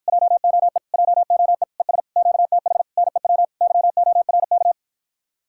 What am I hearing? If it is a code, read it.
Morse code, heard as a keyed tone, about 54 wpm, 700 Hz. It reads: JOE JOE ES QTH NEW YORK